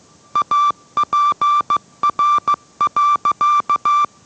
Morse code, heard as a keyed tone, about 15 wpm, 1170 Hz. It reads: APR.